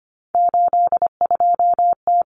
Morse code, heard as a keyed tone, about 25 wpm, 700 Hz. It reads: 82T
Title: Common Cut Numbers - 3-Digit 25wpm